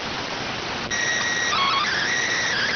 SSTV (slow-scan television), PD 290.